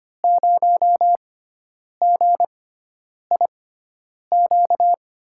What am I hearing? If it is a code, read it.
Morse code, heard as a keyed tone, about 25 wpm, spaced as for 9 wpm, 700 Hz. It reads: 0GIQ
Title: USA Call Signs - 6 Characters - 6x Character Spacing - 25wpm